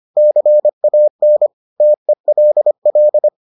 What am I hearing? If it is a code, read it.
Morse code, heard as a keyed tone, about 25 wpm, 600 Hz. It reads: CAN TELL